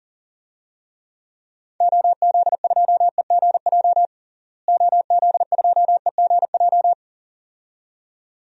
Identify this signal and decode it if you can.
Morse code, heard as a keyed tone, about 37 wpm, 700 Hz. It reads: OZ2EGJ OZ2EGJ
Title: DXCC-100 Call Signs - 6 Characters - Repeated 2x - 40wpm